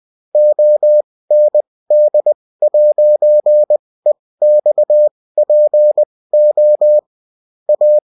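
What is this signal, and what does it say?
Morse code, heard as a keyed tone, about 20 wpm, spaced as for 16 wpm, 600 Hz. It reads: OND'EXPO A
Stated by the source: Morse code bulletin of a French amateur CW union, sent as text and in CW.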